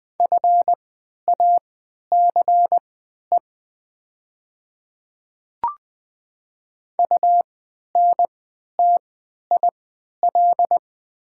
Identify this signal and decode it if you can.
Morse code, heard as a keyed tone, about 20 wpm, spaced as for 11 wpm, 700 Hz. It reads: FACE UNTIL
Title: 3x Character Spacing - Top 300 Words - 20wpm